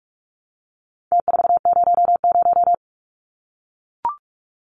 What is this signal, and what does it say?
Morse code, heard as a keyed tone, about 44 wpm, 700 Hz. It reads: T400
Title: Numbers - Contest Style - 4-Digit 45wpm